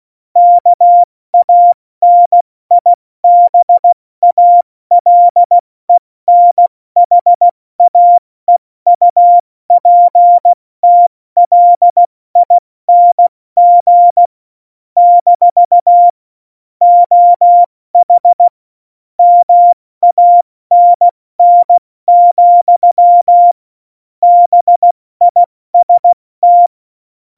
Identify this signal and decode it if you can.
Morse code, heard as a keyed tone, about 16 wpm, 700 Hz. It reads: KANIBALENHAEUPTLING - OH MANN, BIST